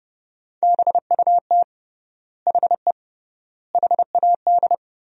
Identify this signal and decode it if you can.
Morse code, heard as a keyed tone, about 30 wpm, 700 Hz. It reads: BUT HE HAD
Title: 3x Word-Spacing - Sentences from Top 100 Words - 30wpm